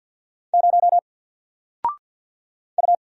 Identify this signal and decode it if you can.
Morse code, heard as a keyed tone, about 50 wpm, 700 Hz. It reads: 0 U
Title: Single Letter-Number 50wpm